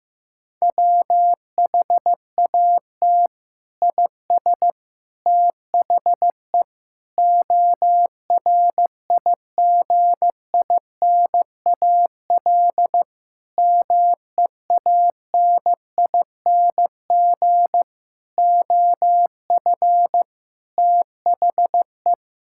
Morse code, heard as a keyed tone, about 15 wpm, 700 Hz. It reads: WHAT IS THE ORIGINAL MEANING OF THE
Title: Sentences from Top 1000 Words - Review Part 6 - 15wpm